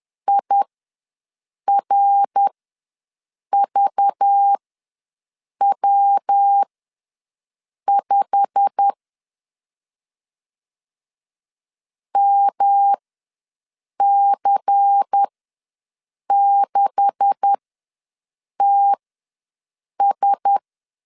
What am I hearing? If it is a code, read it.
Morse code, heard as a keyed tone, about 10 wpm, spaced as for 6 wpm, 800 Hz. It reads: IRVW5 MC6TS
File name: LEZIONI / 9 286 ws 10wpm 4032